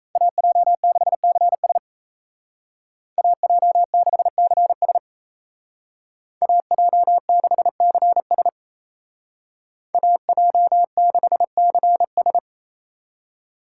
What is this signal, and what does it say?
Morse code, heard as a keyed tone, about 33 wpm, 700 Hz. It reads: AJ6CS AJ6CS AJ6CS AJ6CS
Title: Speed Racing - USA 2x2 Call Signs 28wpm